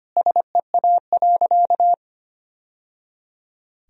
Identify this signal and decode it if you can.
Morse code, heard as a keyed tone, about 25 wpm, 700 Hz. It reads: SEA.
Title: Sentences from Top 1000 Words - Review Part 1 - 25wpm